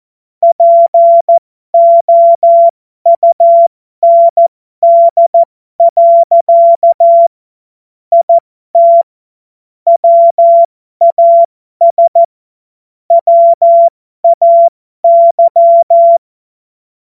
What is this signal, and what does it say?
Morse code, heard as a keyed tone, about 14 wpm, spaced as for 12 wpm, 680 Hz. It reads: POUND. IT WAS WAY
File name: Morsekurs / Witze